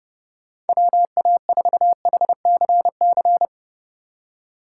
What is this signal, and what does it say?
Morse code, heard as a keyed tone, about 30 wpm, 700 Hz. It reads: WA4HCC
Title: K1USN SST QSO Practice - 30wpm